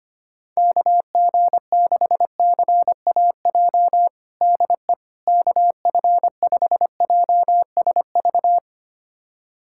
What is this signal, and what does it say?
Morse code, heard as a keyed tone, about 25 wpm, 700 Hz. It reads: KG6CAJ DE KF5JSV
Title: USA Callsign DE Callsign - 6 Characters - 25wpm